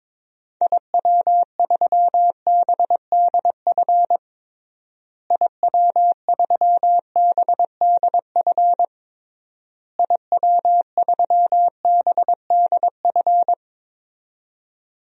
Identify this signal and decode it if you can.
Morse code, heard as a keyed tone, about 22 wpm, 700 Hz. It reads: IW3BDF IW3BDF IW3BDF
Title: DXCC-100 Call Signs - 6 Characters - Repeated 3x - 22wpm